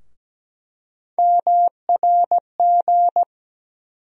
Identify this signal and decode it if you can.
Morse code, heard as a keyed tone, about 17 wpm, 700 Hz. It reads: MRG
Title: USA Call Sign Postfixes - 3 Letters 17wpm